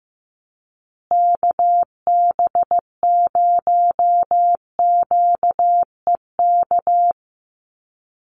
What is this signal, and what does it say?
Morse code, heard as a keyed tone, about 15 wpm, 700 Hz. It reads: KB0QEK
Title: USA Call Signs Repeated 2x 15wpm